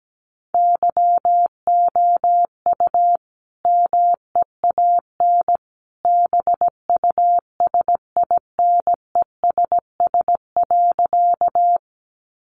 Morse code, heard as a keyed tone, about 17 wpm, 700 Hz. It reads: YOU MEAN BUSINESS.
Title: Sentences from Top 600 Words 17wpm